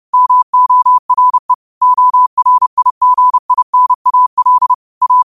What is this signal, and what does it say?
Morse code, heard as a keyed tone, about 30 wpm, 1000 Hz. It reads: MORE ORIGINAL A
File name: qotdfiles / 30wpm